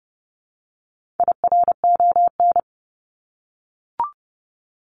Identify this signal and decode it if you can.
Morse code, heard as a keyed tone, about 30 wpm, 700 Hz. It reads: IRON
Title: Top 1751-2000 Words 30wpm